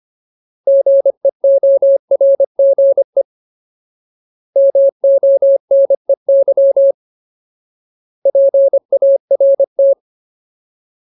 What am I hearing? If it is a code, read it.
Morse code, heard as a keyed tone, about 25 wpm, 550 Hz. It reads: GEORGE MONEY PART
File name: Ultimate Word List / 25WPM Ultimate